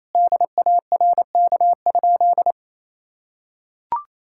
Morse code, heard as a keyed tone, about 28 wpm, 700 Hz. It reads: DARK?
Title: Sentences from Top 1600 Words 28wpm